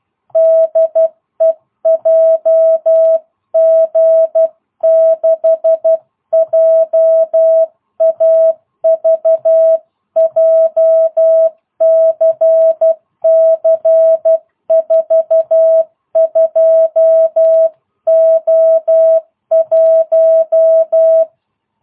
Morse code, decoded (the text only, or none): DE JG6JAV JCC 42O1